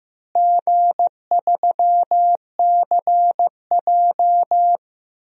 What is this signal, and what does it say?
Morse code, heard as a keyed tone, about 15 wpm, 700 Hz. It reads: G3CJ
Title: DXCC-100 Call Signs - 4 Characters - Repeated 3x - 15wpm